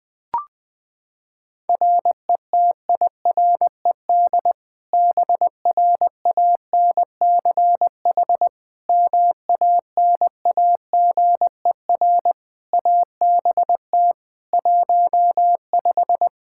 Morse code, heard as a keyed tone, about 20 wpm, 700 Hz. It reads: RETIRED BRANCH MANAGER ABT 15
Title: Practice QSO - Exchange 3 - More Info - 20wpm